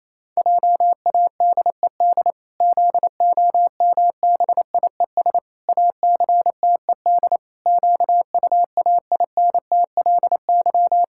Morse code, heard as a keyed tone, about 28 wpm, 700 Hz. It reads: JADED ZOMBIES ACTED QUAINTLY
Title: Pangram Challenge v2 - 28wpm